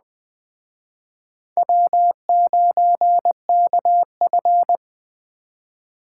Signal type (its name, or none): Morse code